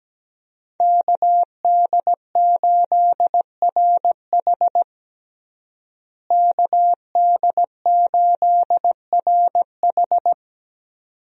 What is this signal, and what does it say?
Morse code, heard as a keyed tone, about 17 wpm, 700 Hz. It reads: KD8RH KD8RH
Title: USA Call Signs - 5 Characters - Repeated 2x - 17wpm